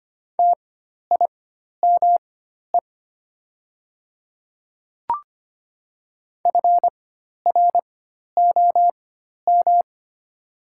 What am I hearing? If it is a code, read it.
Morse code, heard as a keyed tone, about 25 wpm, spaced as for 12 wpm, 700 Hz. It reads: TIME FROM
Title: 4x Character Spacing - Top 100 Words - 25wpm